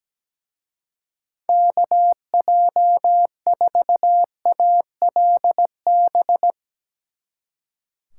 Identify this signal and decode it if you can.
Morse code, heard as a keyed tone, about 17 wpm, 700 Hz. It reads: KJ4ALB
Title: Speed Racing - USA 2x3 Call Signs 17wpm